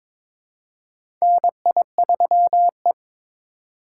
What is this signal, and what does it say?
Morse code, heard as a keyed tone, about 22 wpm, 700 Hz. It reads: NI3E
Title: USA Call Signs - 4 Characters - 2x Character Spacing - 22wpm